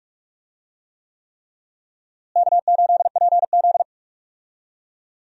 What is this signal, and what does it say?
Morse code, heard as a keyed tone, about 43 wpm, 700 Hz. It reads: K8PZ
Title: USA Call Signs - 4 Characters - Repeated 3x - 45wpm